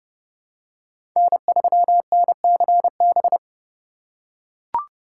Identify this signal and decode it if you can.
Morse code, heard as a keyed tone, about 32 wpm, 700 Hz. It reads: N3NCB
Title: USA Call Signs - 5 Characters - 4x Character Spacing - 30wpm